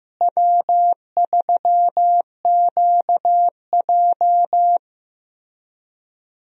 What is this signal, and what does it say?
Morse code, heard as a keyed tone, about 15 wpm, 700 Hz. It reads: W3QJ
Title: USA Call Signs - 4 Characters - Repeated 3x - 15wpm